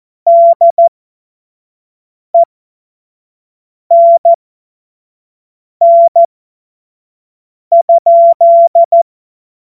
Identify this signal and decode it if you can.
Morse code, heard as a keyed tone, about 14 wpm, spaced as for 5 wpm, 680 Hz. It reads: DENN?
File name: Witz-16-September-2025-25-F70